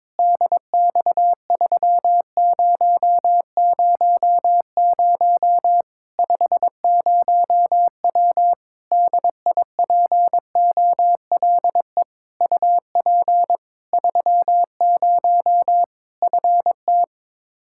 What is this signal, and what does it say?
Morse code, heard as a keyed tone, about 22 wpm, 700 Hz. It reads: DX3000 50W DIPOLE UP 30 FT
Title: Practice QSO - Exchange 2 - Info - 22wpm